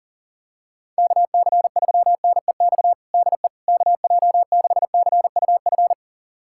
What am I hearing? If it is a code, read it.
Morse code, heard as a keyed tone, about 40 wpm, 700 Hz. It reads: KC3NEX DE KJ6CUF
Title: USA Callsign DE Callsign - 6 Characters - 40wpm